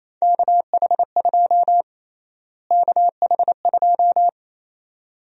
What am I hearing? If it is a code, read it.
Morse code, heard as a keyed tone, about 28 wpm, 700 Hz. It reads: KH2 KH2
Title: USA Call Sign Prefixes - Repeated 2x - 28wpm